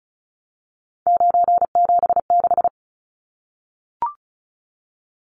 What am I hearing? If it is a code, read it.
Morse code, heard as a keyed tone, about 34 wpm, 700 Hz. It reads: 976
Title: Numbers - 3-Digit 35wpm